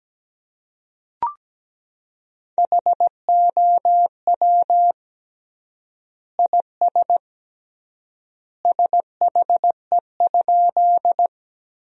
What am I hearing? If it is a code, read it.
Morse code, heard as a keyed tone, about 17 wpm, 700 Hz. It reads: HOW IS SHE?
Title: 3x Word-Spacing - Sentences from Top 100 Words - 17wpm